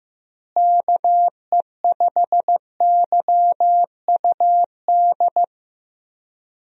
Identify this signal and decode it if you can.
Morse code, heard as a keyed tone, about 15 wpm, 700 Hz. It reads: KE5YUD
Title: USA Call Signs - 6 Characters - Repeated 2x - 15wpm